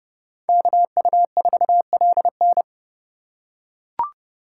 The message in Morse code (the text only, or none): KU4LN